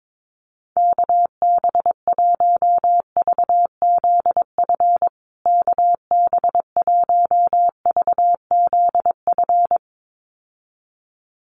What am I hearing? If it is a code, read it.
Morse code, heard as a keyed tone, about 22 wpm, 700 Hz. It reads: KB1VZF KB1VZF
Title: USA Call Signs Repeated 2x 22wpm